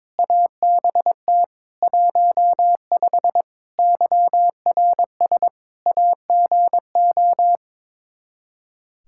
Morse code, heard as a keyed tone, about 22 wpm, 700 Hz. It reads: ABT 15 YRS AGO
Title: Practice QSO - Exchange 3 - More Info - 22wpm